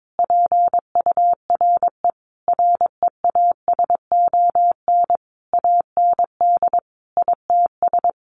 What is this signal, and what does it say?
Morse code, heard as a keyed tone, about 22 wpm, 700 Hz. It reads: PURE REASON AND ITS
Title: Sentences from Top 1500 Words 22wpm